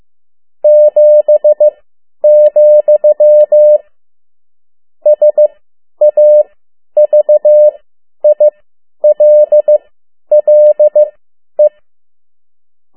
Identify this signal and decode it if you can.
Morse code, heard as a keyed tone, about 15 wpm, spaced as for 10 wpm, 600 Hz. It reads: 7, SAVILLE
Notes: Around the World in 80 Days paragraph one Morse code 10 WPM